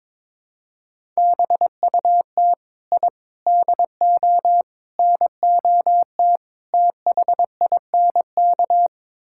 Morse code, heard as a keyed tone, about 22 wpm, 700 Hz. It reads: BUT I DO NOT THINK
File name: practice / 22wpm / Build A Phrase - 5-Word Phrases - 22wpm